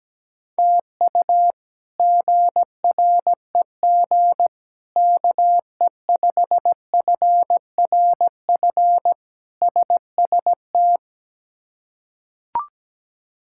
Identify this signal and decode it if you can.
Morse code, heard as a keyed tone, about 17 wpm, 700 Hz. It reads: TU GREG KE5FRF SST
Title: K1USN SST QSO Practice - 17wpm